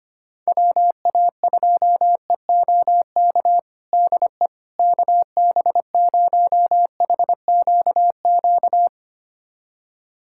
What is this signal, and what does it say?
Morse code, heard as a keyed tone, about 25 wpm, 700 Hz. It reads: WA2EOK DE KB0HQQ
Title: USA Callsign DE Callsign - 6 Characters - 25wpm